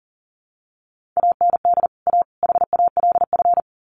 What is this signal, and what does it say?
Morse code, heard as a keyed tone, about 40 wpm, 700 Hz. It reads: AND A HALF
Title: Sets of 3 Words 40wpm